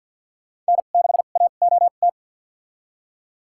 Morse code, heard as a keyed tone, about 52 wpm, spaced as for 36 wpm, 700 Hz. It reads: N6AOT
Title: USA Call Signs - 5 Characters - 2x Character Spacing - 50wpm